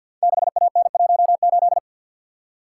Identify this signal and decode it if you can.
Morse code, heard as a keyed tone, about 51 wpm, 700 Hz. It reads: 6AN18
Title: Code Groups - 5 Letter:Numbers - 50wpm